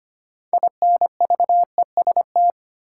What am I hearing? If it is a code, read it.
Morse code, heard as a keyed tone, about 25 wpm, 700 Hz. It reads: INVEST
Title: Sentences from Top 1100 Words - New Word Drill Alt - 25wpm